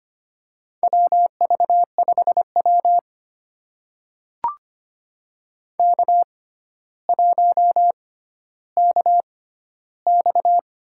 Morse code, heard as a keyed tone, about 25 wpm, spaced as for 13 wpm, 700 Hz. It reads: WV5W K1KX
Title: USA Call Signs - 4 Characters - 6x Character Spacing - 25wpm